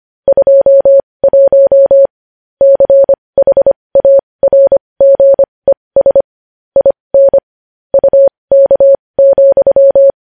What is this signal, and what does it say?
Morse code, heard as a keyed tone, about 25 wpm, spaced as for 20 wpm, 550 Hz. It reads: 21 CHARGES IN UK,
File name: podcast / News headlines-1x-25x20